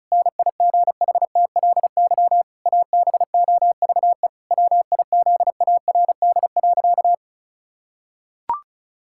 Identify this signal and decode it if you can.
Morse code, heard as a keyed tone, about 36 wpm, 700 Hz. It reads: NIGHTLY ABOVE WIZARD.
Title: Pangram Challenge v2 - 35wpm